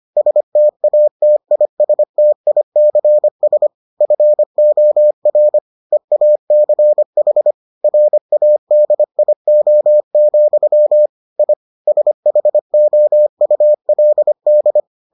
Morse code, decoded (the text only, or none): STATISTICS FOR EACH RADIO, I SHOULD